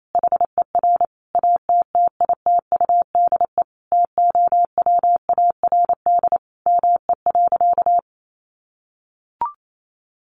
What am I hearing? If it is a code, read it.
Morse code, heard as a keyed tone, about 28 wpm, 700 Hz. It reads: HER ATTITUDE TOWARD ME.
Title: Sentences from Top 1300 Words 28wpm